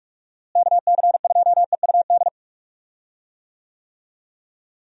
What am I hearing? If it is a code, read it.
Morse code, heard as a keyed tone, about 46 wpm, 700 Hz. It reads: KC2EUD
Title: USA Call Signs - 6 Characters - Repeated 2x - 45wpm